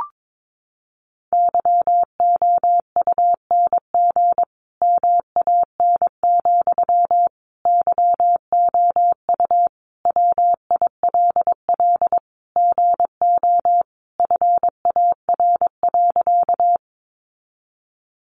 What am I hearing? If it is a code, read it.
Morse code, heard as a keyed tone, about 22 wpm, 700 Hz. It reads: YOUNG MAN, YOU WILL GO FAR.
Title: Sentences from Top 400 Words 22wpm